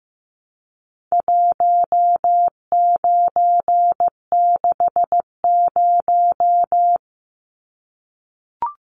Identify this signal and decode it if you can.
Morse code, heard as a keyed tone, about 15 wpm, 700 Hz. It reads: 1960
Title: Numbers - 4-Digit 15wpm